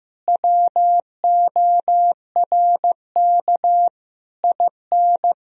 Morse code, heard as a keyed tone, about 15 wpm, 700 Hz. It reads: WORK IN
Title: Sentences from Top 1000 Words - Review Part 4 - 15wpm